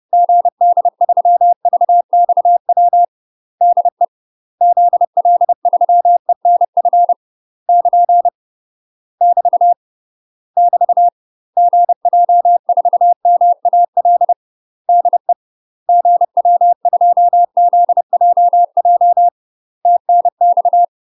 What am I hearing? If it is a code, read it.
Morse code, heard as a keyed tone, about 30 wpm, 700 Hz. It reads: GD3VXW DE ZL3ENF <KN> <BT> <BT> GJ4MAL DE GW2ZJJ TNX
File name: UK MASTER QSOs TRAINING / 30WPM / UK MASTER QSOs 0022